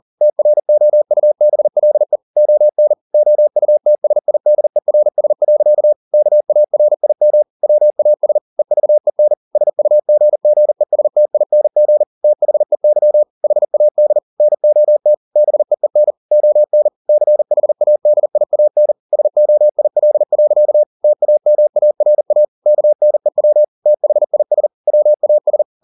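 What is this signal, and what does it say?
Morse code, heard as a keyed tone, about 40 wpm, 600 Hz. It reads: TROUBLE ON OUTSIDERS. KARIM WAS EVEN SUGGESTING THEY HAD NOT BEEN ON CHADIAN SOIL. TAMARA KNEW THIS WAS
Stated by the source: Morse code at 40 wpm, practice text, CW med Gnister.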